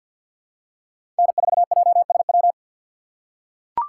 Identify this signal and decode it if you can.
Morse code, heard as a keyed tone, about 46 wpm, 700 Hz. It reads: N4JSW
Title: USA Call Signs - 5 Characters - 4x Character Spacing - 50wpm